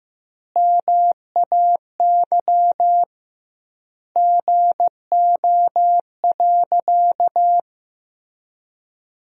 Morse code, heard as a keyed tone, about 15 wpm, 700 Hz. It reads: MAY GO.
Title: 2x Word-Spacing - Sentences from Top 100 Words - 15wpm